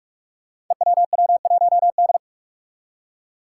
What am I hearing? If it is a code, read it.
Morse code, heard as a keyed tone, about 41 wpm, 700 Hz. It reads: EWW1D
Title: Five Challenging Letter-Number Combinations 45wpm